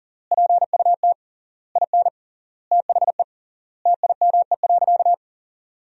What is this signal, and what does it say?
Morse code, heard as a keyed tone, about 37 wpm, 700 Hz. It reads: PUT IN THE TIME.
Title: 3x Word-Spacing - Sentences from Top 200 Words - 40wpm